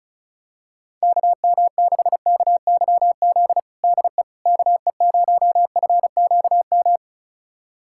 Morse code, heard as a keyed tone, about 36 wpm, 700 Hz. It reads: KM6KYZ DE KE0FQM
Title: USA Callsign DE Callsign 35wpm